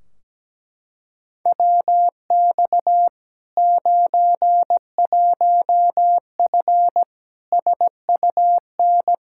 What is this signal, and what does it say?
Morse code, heard as a keyed tone, about 17 wpm, 700 Hz. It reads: WX 91F SUN
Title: Practice QSO - Exchange 2 - Info - 17wpm